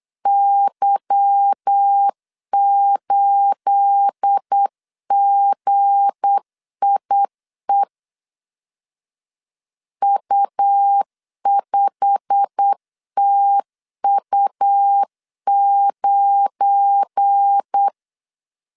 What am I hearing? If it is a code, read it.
Morse code, heard as a keyed tone, about 8 wpm, 800 Hz. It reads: Y8GIE U5TU9
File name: CW / DET / 1 9 1500C S B 8wpm 5624